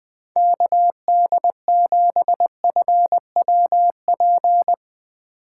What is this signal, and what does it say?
Morse code, heard as a keyed tone, about 21 wpm, 700 Hz. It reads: KD7FWP